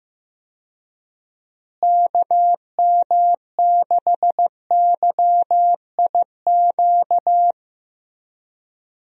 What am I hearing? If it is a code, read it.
Morse code, heard as a keyed tone, about 15 wpm, 700 Hz. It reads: KM6YIQ
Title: USA Call Signs - 6 Characters - Repeated 2x - 15wpm